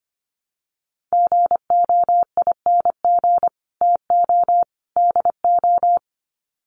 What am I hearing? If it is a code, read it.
Morse code, heard as a keyed tone, about 25 wpm, 700 Hz. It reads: GOING TO DO
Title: Sets of 3 Words 25wpm